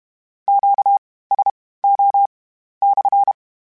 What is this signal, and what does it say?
Morse code, heard as a keyed tone, about 32 wpm, spaced as for 18 wpm, 800 Hz. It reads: QSO /
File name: Q Calls / Q Calls-mp3s / Q Calls SPSL2 @30